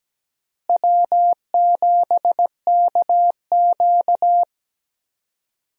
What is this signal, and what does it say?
Morse code, heard as a keyed tone, about 17 wpm, 700 Hz. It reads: W7KQ
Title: USA Call Signs - 4 Characters - Repeated 3x - 17wpm